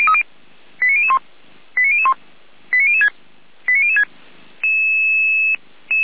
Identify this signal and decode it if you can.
Five-tone selective call (ZVEI), heard as quick runs of five tones, tones of about 70 ms. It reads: 89001 89001 89006 89006